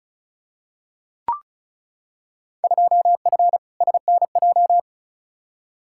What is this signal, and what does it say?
Morse code, heard as a keyed tone, about 34 wpm, 700 Hz. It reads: 2F SNJ
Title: Field Day Exchanges Encore - 35wpm